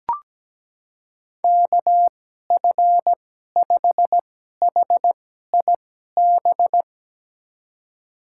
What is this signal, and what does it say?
Morse code, heard as a keyed tone, about 17 wpm, spaced as for 12 wpm, 700 Hz. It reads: KF5HIB
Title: USA Call Signs - 6 Characters - 2x Character Spacing - 17wpm